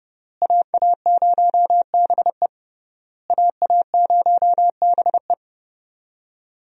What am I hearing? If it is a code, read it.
Morse code, heard as a keyed tone, about 29 wpm, 700 Hz. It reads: AA0BE AA0BE
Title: USA Call Signs - 5 Characters - Repeated 3x - 30wpm